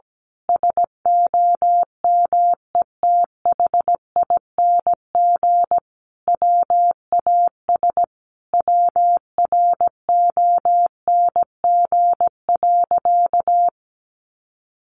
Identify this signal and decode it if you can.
Morse code, heard as a keyed tone, about 17 wpm, 700 Hz. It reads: SOMETHING WAS WRONG.
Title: Sentences from Top 1200 Words 17wpm